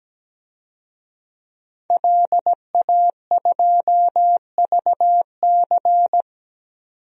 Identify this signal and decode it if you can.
Morse code, heard as a keyed tone, about 17 wpm, 700 Hz. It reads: LA2VC
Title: DXCC-100 Call Signs - 5 Characters - 17wpm